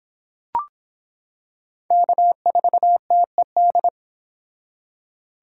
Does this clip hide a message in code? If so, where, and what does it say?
Yes: Morse code from 1.9 to 3.9 s, heard as a keyed tone, about 26 wpm, 700 Hz. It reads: K4TED